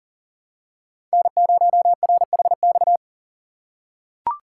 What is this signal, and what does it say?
Morse code, heard as a keyed tone, about 37 wpm, 700 Hz. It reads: N0RHX
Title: USA Call Signs - 5 Characters - 6x Character Spacing - 40wpm